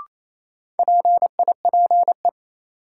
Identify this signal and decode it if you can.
Morse code, heard as a keyed tone, about 28 wpm, 700 Hz. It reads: PIPE
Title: Sentences from Top 1500 Words - New Word Drill - 28wpm